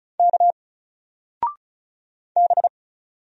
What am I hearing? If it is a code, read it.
Morse code, heard as a keyed tone, about 36 wpm, 700 Hz. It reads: K B
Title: Single Character (Letter:Number:Punctuation:Prosign) 35wpm